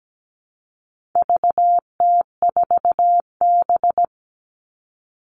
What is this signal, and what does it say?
Morse code, heard as a keyed tone, about 17 wpm, 700 Hz. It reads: VT4B
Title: All Cut Numbers - 4-Digit 17wpm